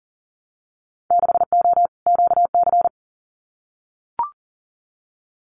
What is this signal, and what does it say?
Morse code, heard as a keyed tone, about 42 wpm, 700 Hz. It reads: 6O QC
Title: Winter Field Day Exchanges Encore 40wpm